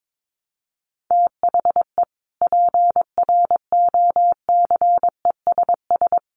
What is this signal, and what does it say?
Morse code, heard as a keyed tone, about 22 wpm, 700 Hz. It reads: THE PROCESS
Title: Sets of 3 Words 22wpm